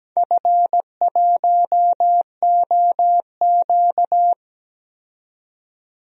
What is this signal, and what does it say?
Morse code, heard as a keyed tone, about 17 wpm, 700 Hz. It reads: F1OQ
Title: DXCC-100 Call Signs - 4 Characters - 17wpm